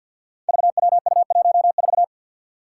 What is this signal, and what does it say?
Morse code, heard as a keyed tone, about 53 wpm, 700 Hz. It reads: VWU14
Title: Five Challenging Letter-Number Combinations 50wpm